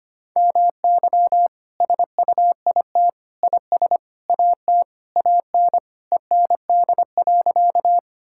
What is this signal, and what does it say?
Morse code, heard as a keyed tone, about 25 wpm, 700 Hz. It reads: MY SUIT IS AT AN END.
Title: Sentences from Top 1600 Words 25wpm